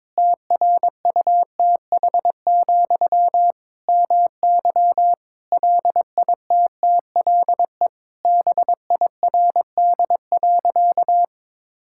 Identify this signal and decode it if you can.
Morse code, heard as a keyed tone, about 22 wpm, 700 Hz. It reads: TRUTH, MY LITTLE BIRD.
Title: Sentences from Top 1000 Words - Review Part 6 - 22wpm